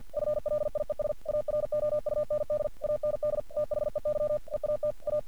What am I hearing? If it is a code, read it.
Morse code, heard as a keyed tone, about 46 wpm, 600 Hz. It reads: FLIES AROUND AND THEY EAT A
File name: archived / CRSnet